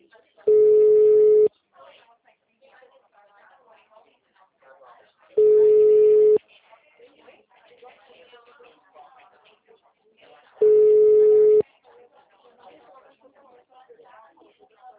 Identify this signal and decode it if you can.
Ringback tone (Europe, 425 Hz), with speech.